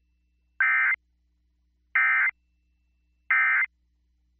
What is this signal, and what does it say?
Emergency Alert System (SAME) tones, heard as harsh buzzing bursts. It is heard from 0.6 to 3.7 s.